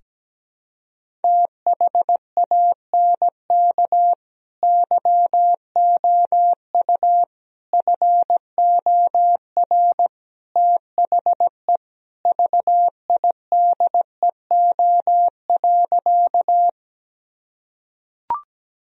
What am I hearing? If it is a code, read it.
Morse code, heard as a keyed tone, about 17 wpm, 700 Hz. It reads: THANK YOU FOR THE VIDEO.
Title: Sentences from Top 1000 Words - Review Part 4 - 17wpm